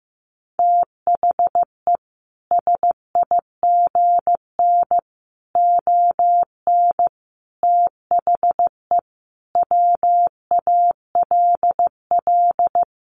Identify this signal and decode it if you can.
Morse code, heard as a keyed tone, about 15 wpm, 700 Hz. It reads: THE SIGN ON THE WALL
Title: A Sound Of Thunder 15wpm